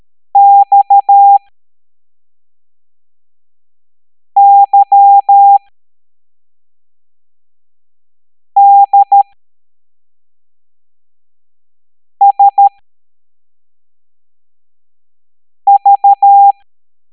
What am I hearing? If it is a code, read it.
Morse code, heard as a keyed tone, about 13 wpm, spaced as for 3 wpm, 800 Hz. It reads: XYDSV
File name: hamdata / L05-EX01-SVY